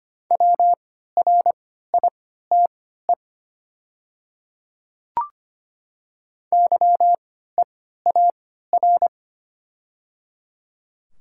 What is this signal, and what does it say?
Morse code, heard as a keyed tone, about 25 wpm, spaced as for 14 wpm, 700 Hz. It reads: WRITE YEAR
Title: 3x Character Spacing - Top 200 Words - 25wpm